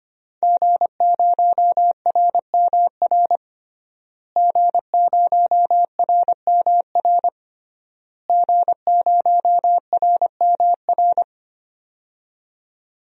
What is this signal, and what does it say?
Morse code, heard as a keyed tone, about 25 wpm, 700 Hz. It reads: G0RMR G0RMR G0RMR
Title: DXCC-100 Call Signs - 5 Characters - Repeated 3x - 25wpm